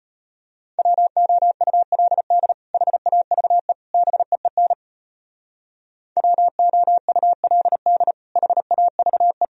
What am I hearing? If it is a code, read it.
Morse code, heard as a keyed tone, about 30 wpm, 700 Hz. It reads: WOULD HAVE BEEN WOULD HAVE
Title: Speed-Racing - Sets of 3 Words 25wpm